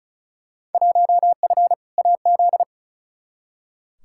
Morse code, heard as a keyed tone, about 36 wpm, 700 Hz. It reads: 1F AZ
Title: Field Day Exchanges - 35wpm